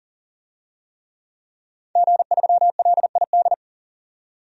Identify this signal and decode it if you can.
Morse code, heard as a keyed tone, about 40 wpm, 700 Hz. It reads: G3LID
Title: DXCC-100 Call Signs - 5 Characters - 40wpm